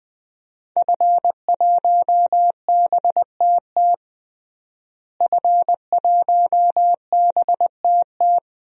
Morse code, heard as a keyed tone, about 20 wpm, 700 Hz. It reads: F1BTT F1BTT
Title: DXCC-100 Call Signs - 5 Characters - Repeated 3x - 20wpm